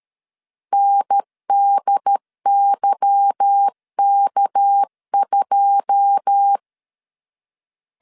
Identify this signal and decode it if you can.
Morse code, heard as a keyed tone, about 13 wpm, 800 Hz. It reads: NDYK2
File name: LEZIONI / 9 2 12wpm 2301